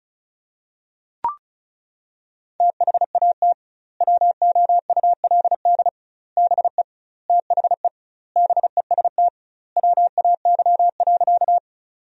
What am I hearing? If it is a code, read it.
Morse code, heard as a keyed tone, about 36 wpm, 700 Hz. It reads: THAT WOULD BE THE BEST WAY.
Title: 2x Word-Spacing - Sentences from Top 400 Words - 35wpm